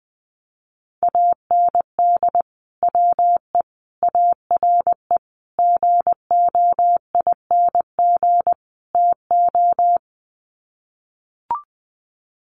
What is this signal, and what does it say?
Morse code, heard as a keyed tone, about 20 wpm, 700 Hz. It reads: AND WE ARE GOING TO
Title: Sets of 5 Words Encore2 20wpm